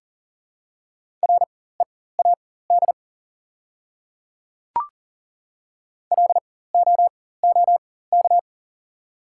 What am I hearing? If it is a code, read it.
Morse code, heard as a keyed tone, about 40 wpm, spaced as for 19 wpm, 690 Hz. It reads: READ LOOK
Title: 4x Character Spacing - Top 200 Words - 40wpm